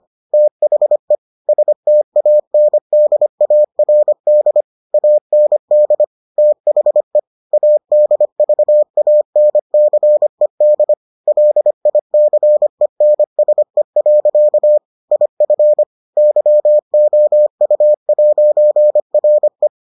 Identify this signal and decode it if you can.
Morse code, heard as a keyed tone, about 25 wpm, 600 Hz. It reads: THE STANDARD AND THE ADVANCED LICENSE. IF YOU'RE